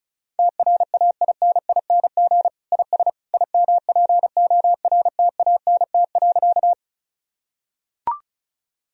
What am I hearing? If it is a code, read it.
Morse code, heard as a keyed tone, about 36 wpm, 700 Hz. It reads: TRAINING IS IMPORTANT.
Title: Sentences from Top 1000 Words - Review Part 4 - 35wpm